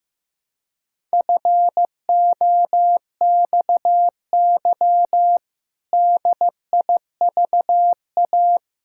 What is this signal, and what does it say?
Morse code, heard as a keyed tone, about 16 wpm, 700 Hz. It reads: FOXY DIVA
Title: Pangram Challenge v2 - 15wpm